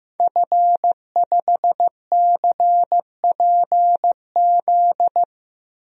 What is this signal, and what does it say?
Morse code, heard as a keyed tone, about 15 wpm, 700 Hz. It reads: F5CPZ
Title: DXCC-100 Call Signs - 5 Characters - 15wpm